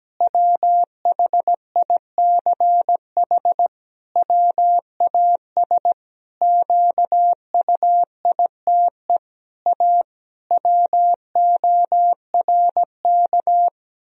Morse code, heard as a keyed tone, about 17 wpm, 700 Hz. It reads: WHICH WAS QUITE A WORK
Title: Sentences from Top 1000 Words - Review Part 3 - 17wpm